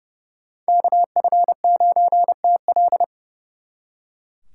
Morse code, heard as a keyed tone, about 35 wpm, 700 Hz. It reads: KF9TL
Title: Speed Racing - USA 2x2 Call Signs 30wpm